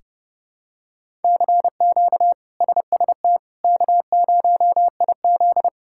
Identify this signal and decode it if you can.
Morse code, heard as a keyed tone, about 30 wpm, 700 Hz. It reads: CQ SST K0IZ
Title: K1USN SST QSO Practice - 30wpm